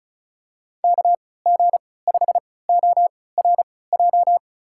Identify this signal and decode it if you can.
Morse code, heard as a keyed tone, about 36 wpm, spaced as for 20 wpm, 700 Hz. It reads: KG5ORJ
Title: USA Call Signs - 3x Character Spacing - 35wpm